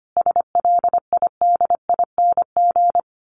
Morse code, heard as a keyed tone, about 25 wpm, 700 Hz. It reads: SLIDING
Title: A Sound Of Thunder 25wpm